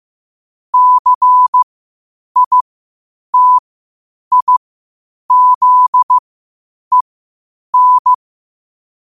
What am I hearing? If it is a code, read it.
Morse code, heard as a keyed tone, about 15 wpm, spaced as for 8 wpm, 1000 Hz. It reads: CITIZEN
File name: qotdfiles / 8wpm